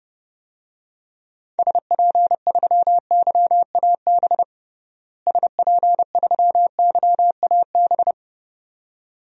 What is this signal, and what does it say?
Morse code, heard as a keyed tone, about 30 wpm, 700 Hz. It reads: SP3YAB SP3YAB
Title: DXCC-100 Call Signs - 6 Characters - Repeated 2x - 30wpm